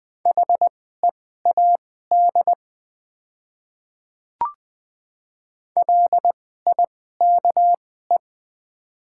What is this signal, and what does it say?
Morse code, heard as a keyed tone, about 20 wpm, spaced as for 14 wpm, 700 Hz. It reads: HEAD LIKE
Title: 2x Character Spacing - Top 500 Words - 20wpm